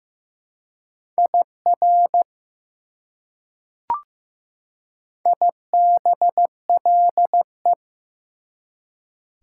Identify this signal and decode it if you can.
Morse code, heard as a keyed tone, about 15 wpm, 700 Hz. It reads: IR IBLE
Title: Prefixes and Suffixes 15wpm